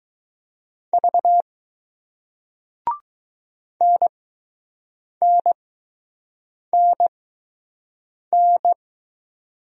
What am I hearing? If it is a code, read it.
Morse code, heard as a keyed tone, about 20 wpm, spaced as for 6 wpm, 700 Hz. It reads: V NNNN